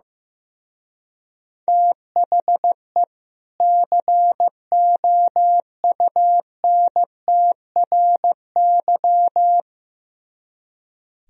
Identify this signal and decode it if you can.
Morse code, heard as a keyed tone, about 17 wpm, 700 Hz. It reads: THE COUNTRY
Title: Speed-Racing - Sets of 2 Words 15wpm